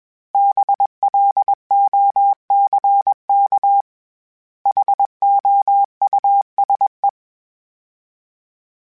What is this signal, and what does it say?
Morse code, heard as a keyed tone, about 21 wpm, 800 Hz. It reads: BLOCK HOUSE
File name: General / General-mp3s / Houses @20